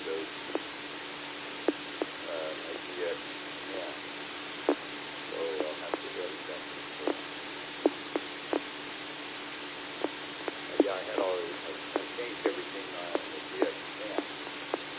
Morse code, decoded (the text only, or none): none